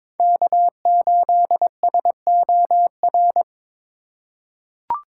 Morse code, heard as a keyed tone, about 22 wpm, 700 Hz. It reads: K8SOR
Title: USA Call Signs - 3x Character Spacing - 22wpm